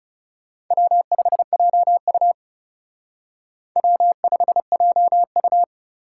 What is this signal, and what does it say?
Morse code, heard as a keyed tone, about 37 wpm, 700 Hz. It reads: W5JU W5JU
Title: Speed Racing - USA 1x2 Call Signs 30wpm